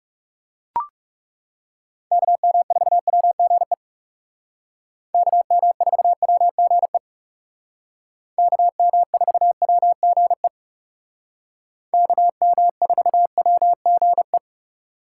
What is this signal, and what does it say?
Morse code, heard as a keyed tone, about 38 wpm, 700 Hz. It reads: KM4WGE KM4WGE KM4WGE KM4WGE